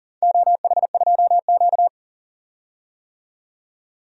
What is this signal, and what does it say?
Morse code, heard as a keyed tone, about 40 wpm, 700 Hz. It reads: OH2Q
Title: DXCC-100 Call Signs - 4 Characters - 40wpm